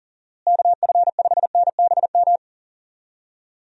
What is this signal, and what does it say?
Morse code, heard as a keyed tone, about 40 wpm, 700 Hz. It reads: KF5NBM
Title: USA Call Signs - 6 Characters - Repeated 2x - 40wpm